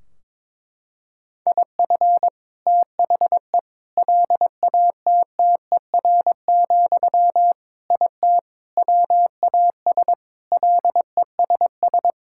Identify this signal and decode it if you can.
Morse code, heard as a keyed tone, about 22 wpm, 700 Hz. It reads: IF THE LATTER, IT WAS LESS
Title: Sentences from Top 1600 Words 22wpm